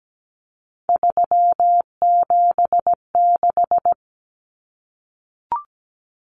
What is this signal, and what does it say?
Morse code, heard as a keyed tone, about 17 wpm, 700 Hz. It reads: 376
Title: Common Cut Numbers - 3-Digit 17wpm